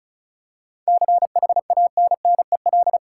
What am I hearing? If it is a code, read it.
Morse code, heard as a keyed tone, about 36 wpm, 700 Hz. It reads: CHANNEL
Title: Seven Letter Words - 35wpm